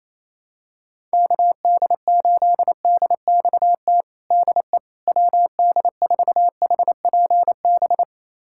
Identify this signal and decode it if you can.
Morse code, heard as a keyed tone, about 28 wpm, 700 Hz. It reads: KD8DXT DE WD4HPB
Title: USA Callsign DE Callsign - 6 Characters - 28wpm